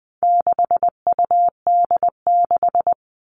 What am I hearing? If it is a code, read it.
Morse code, heard as a keyed tone, about 20 wpm, 700 Hz. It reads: 6UD6
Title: All Cut Numbers - 4-Digit 20wpm